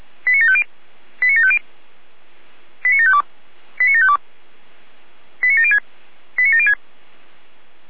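Five-tone selective call (ZVEI), heard as quick runs of five tones, tones of about 80 ms. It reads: 89750 89750 89752 89752 89706 89706
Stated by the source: ZVEI five-tone call received with an SDR.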